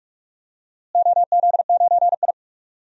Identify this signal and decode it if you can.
Morse code, heard as a keyed tone, about 46 wpm, 700 Hz. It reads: OZ9I
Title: DXCC-100 Call Signs - 4 Characters - Repeated 2x - 45wpm